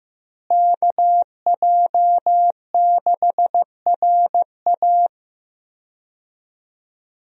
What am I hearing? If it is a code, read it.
Morse code, heard as a keyed tone, about 15 wpm, 700 Hz. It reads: KJ6RA
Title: USA Call Signs - 5 Characters - Repeated 3x - 15wpm